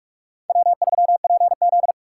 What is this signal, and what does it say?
Morse code, heard as a keyed tone, about 60 wpm, spaced as for 52 wpm, 700 Hz. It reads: W3PZ
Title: Speed Racing - USA 1x2 Call Signs 45wpm